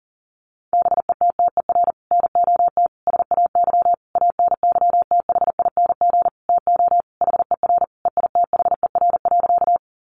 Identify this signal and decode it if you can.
Morse code, heard as a keyed tone, about 40 wpm, 700 Hz. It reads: BETTER NOT SAY ANYTHING TO HER EITHER.
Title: Sentences from Top 1000 Words 40wpm